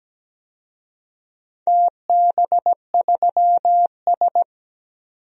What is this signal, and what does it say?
Morse code, heard as a keyed tone, about 17 wpm, 700 Hz. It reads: TB3S
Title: DXCC-100 Call Signs - 4 Characters - 17wpm